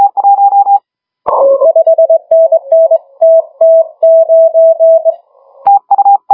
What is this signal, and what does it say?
Morse code, heard as a keyed tone, about 20 wpm, 650 Hz. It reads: 5NN TT9
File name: HG1R CW SSB MIX